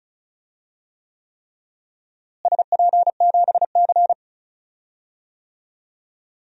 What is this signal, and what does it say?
Morse code, heard as a keyed tone, about 36 wpm, 700 Hz. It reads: SP7C
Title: DXCC-100 Call Signs - 4 Characters - Repeated 3x - 35wpm